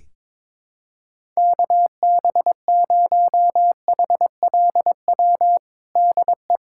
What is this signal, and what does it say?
Morse code, heard as a keyed tone, about 22 wpm, 700 Hz. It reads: KB0HLW DE
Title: USA Callsign DE Callsign - 6 Characters - 22wpm